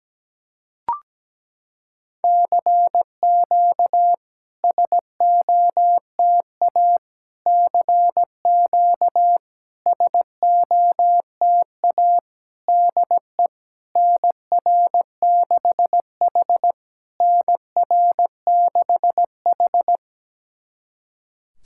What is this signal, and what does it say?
Morse code, heard as a keyed tone, about 17 wpm, 690 Hz. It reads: CQ SOTA CQ SOTA DE NR6H NR6H
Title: SOTA QSO Practice 17wpm